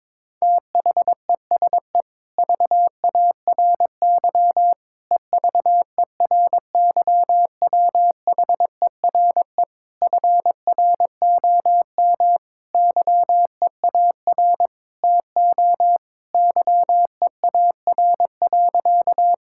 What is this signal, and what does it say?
Morse code, heard as a keyed tone, about 22 wpm, 700 Hz. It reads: THESE VARY EVERYWHERE FROM YEAR TO YEAR.
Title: Sentences from Top 1600 Words 22wpm